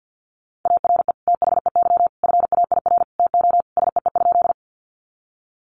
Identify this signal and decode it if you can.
Morse code, heard as a keyed tone, about 50 wpm, 700 Hz. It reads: ARE THEY FAIR TO SEE?
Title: Sentences from Top 1300 Words 50wpm